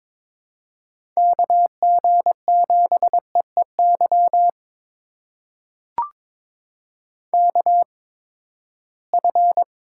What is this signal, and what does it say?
Morse code, heard as a keyed tone, about 22 wpm, spaced as for 15 wpm, 700 Hz. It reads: KG7EEY KF
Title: USA Call Signs - 8x Character Spacing - 22wpm